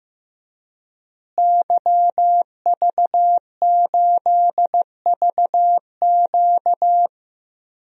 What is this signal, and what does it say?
Morse code, heard as a keyed tone, about 15 wpm, 700 Hz. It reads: YV8VQ
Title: DXCC-100 Call Signs - 5 Characters - Repeated 2x - 15wpm